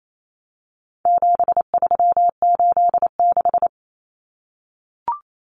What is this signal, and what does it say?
Morse code, heard as a keyed tone, about 28 wpm, 700 Hz. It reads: 7386
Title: Common Cut Numbers - 4-Digit 28wpm